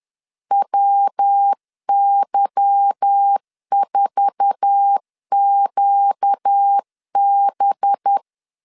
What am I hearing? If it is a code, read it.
Morse code, heard as a keyed tone, about 10 wpm, 800 Hz. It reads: WY4QB
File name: LEZIONI / 8 YJ45 / 8 YJ4 10wpm 2633